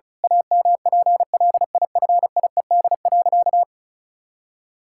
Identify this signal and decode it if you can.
Morse code, heard as a keyed tone, about 36 wpm, 700 Hz. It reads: AMPLIFIED.
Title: Pangram Challenge v2 - 35wpm